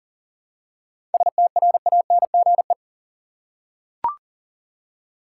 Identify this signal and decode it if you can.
Morse code, heard as a keyed tone, about 40 wpm, 700 Hz. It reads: STRANGE
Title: Seven Letter Words - 40wpm